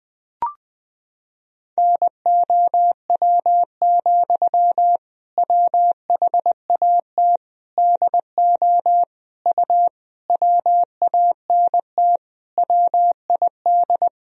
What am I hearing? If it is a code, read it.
Morse code, heard as a keyed tone, about 20 wpm, 700 Hz. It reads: NOW, WHAT DO U WANT WID
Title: Sentences from Top 200 Words and Abbreviations 20wpm